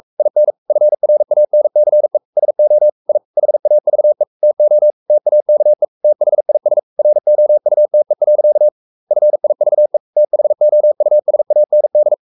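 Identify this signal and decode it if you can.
Morse code, heard as a keyed tone, about 44 wpm, 600 Hz. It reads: IN FRANCE SO I HAVE TO TAKE THIS ROUTE. FIVE THOUSAND
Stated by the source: Morse code at 43 wpm sending Ken Follett's novel Never in English, from CW med Gnister.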